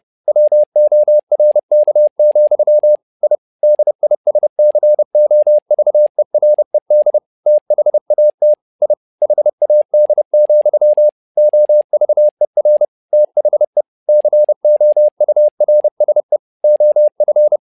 Morse code, heard as a keyed tone, about 30 wpm, 600 Hz. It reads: WORK, I DISCOVERED THAT I HAD, OVER THE COURSE OF
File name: learning center / tower trouble / tower trouble 30wpm